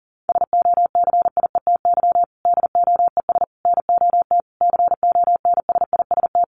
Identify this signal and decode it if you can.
Morse code, heard as a keyed tone, about 40 wpm, 700 Hz. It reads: SOCIETY DOES NOT CONSIST
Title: Sentences from Top 1400 Words 40wpm